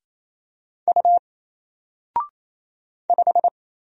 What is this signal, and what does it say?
Morse code, heard as a keyed tone, about 27 wpm, 700 Hz. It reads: U 5
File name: Single-Letter-Number-28wpm